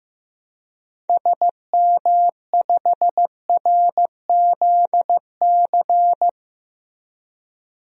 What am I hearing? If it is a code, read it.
Morse code, heard as a keyed tone, about 15 wpm, 700 Hz. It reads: SM5RZC
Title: DXCC-100 Call Signs - 6 Characters - Repeated 2x - 15wpm